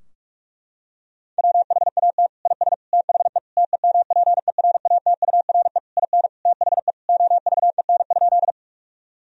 Morse code, heard as a keyed tone, about 46 wpm, 700 Hz. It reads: WHAT IS THE TEMPERATURE IN THE OVEN?
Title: Sentences from Top 1000 Words - Review Part 4 - 45wpm